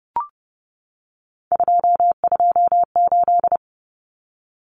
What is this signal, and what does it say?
Morse code, heard as a keyed tone, about 30 wpm, 700 Hz. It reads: 228
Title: Numbers - 3-Digit 30wpm